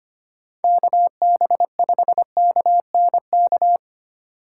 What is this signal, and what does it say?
Morse code, heard as a keyed tone, about 25 wpm, 700 Hz. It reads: KB5KNK
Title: USA Call Signs - 6 Characters - 6x Character Spacing - 25wpm